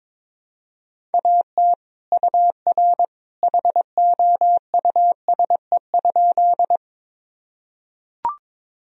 Morse code, heard as a keyed tone, about 22 wpm, 700 Hz. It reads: AT UR HOUSE?
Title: Sentences from Top 200 Words and Abbreviations 22wpm